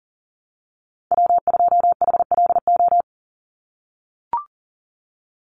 Morse code, heard as a keyed tone, about 39 wpm, 690 Hz. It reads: W2HLO
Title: USA Call Signs 40wpm